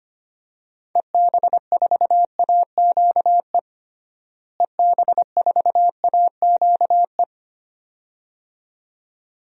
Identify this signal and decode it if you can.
Morse code, heard as a keyed tone, about 25 wpm, 700 Hz. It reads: EB4AQE EB4AQE
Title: DXCC-100 Call Signs - 6 Characters - Repeated 2x - 25wpm